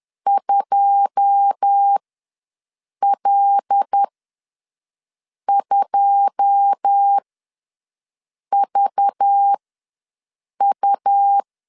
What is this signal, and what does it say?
Morse code, heard as a keyed tone, about 10 wpm, spaced as for 5 wpm, 800 Hz. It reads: 2L2VU
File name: LEZIONI / 9 2 ws 10wpm 4130